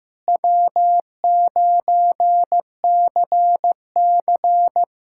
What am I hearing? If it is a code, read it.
Morse code, heard as a keyed tone, about 15 wpm, 700 Hz. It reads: W9CC
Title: USA Callsign DE Callsign 15wpm